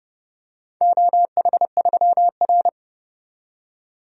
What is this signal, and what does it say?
Morse code, heard as a keyed tone, about 32 wpm, 700 Hz. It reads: OH3R
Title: DXCC-100 Call Signs - 4 Characters - 30wpm